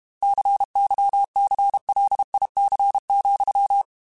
Morse code, heard as a keyed tone, about 32 wpm, 800 Hz. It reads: CYCLIC,